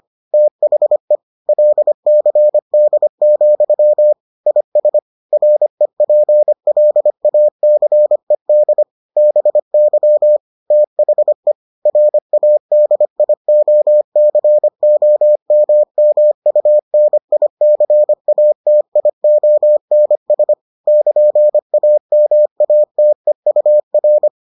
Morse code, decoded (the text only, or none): THE LCD, IS REPLACED BY THE RADIOCOMMUNICATIONS <KN>AMATEUR